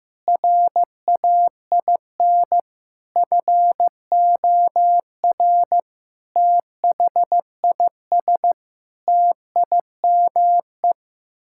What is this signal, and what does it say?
Morse code, heard as a keyed tone, about 15 wpm, 700 Hz. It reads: RAIN FOR THIS TIME
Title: Sentences from Top 1000 Words - Review Part 6 - 15wpm